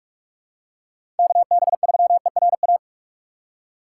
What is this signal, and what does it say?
Morse code, heard as a keyed tone, about 47 wpm, 700 Hz. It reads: KB3ERA
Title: USA Call Signs - 6x Character Spacing - 45wpm